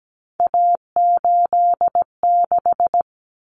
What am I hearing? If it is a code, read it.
Morse code, heard as a keyed tone, about 17 wpm, 700 Hz. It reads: A86
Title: Common Cut Numbers - 3-Digit 17wpm